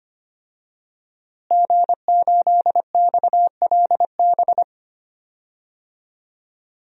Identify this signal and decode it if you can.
Morse code, heard as a keyed tone, about 25 wpm, 700 Hz. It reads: G8XLB
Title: DXCC-100 Call Signs - 5 Characters - Repeated 3x - 25wpm